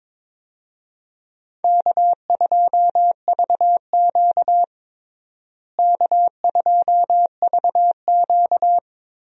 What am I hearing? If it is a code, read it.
Morse code, heard as a keyed tone, about 22 wpm, 700 Hz. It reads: K2VQ K2VQ
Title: USA Call Signs - 4 Characters - Repeated 2x - 22wpm